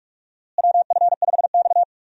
Morse code, heard as a keyed tone, about 46 wpm, 700 Hz. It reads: WF5X
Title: USA Call Signs - 4 Characters - Repeated 3x - 45wpm